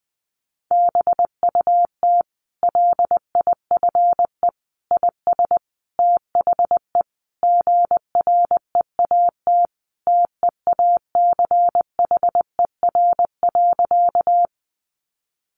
Morse code, heard as a keyed tone, about 19 wpm, 700 Hz. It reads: BUT LIFE IS THE GREAT TEACHER.
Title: Sentences from Top 900 Words 20wpm